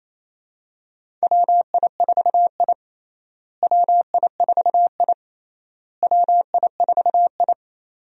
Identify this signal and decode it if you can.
Morse code, heard as a keyed tone, about 28 wpm, 700 Hz. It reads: WI4I WI4I WI4I
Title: USA Call Signs - 4 Characters - Repeated 3x - 28wpm